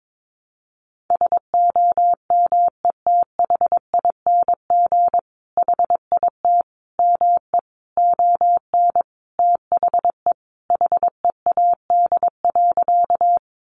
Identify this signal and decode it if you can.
Morse code, heard as a keyed tone, about 22 wpm, 700 Hz. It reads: SOMETHING HIT ME ON THE HEAD.
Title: Sentences from Top 1200 Words 22wpm